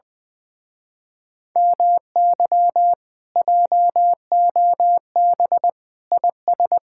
Morse code, heard as a keyed tone, about 20 wpm, 700 Hz. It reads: MY JOB IS
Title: Speed-Racing Common QSO Phrases Encore - 20wpm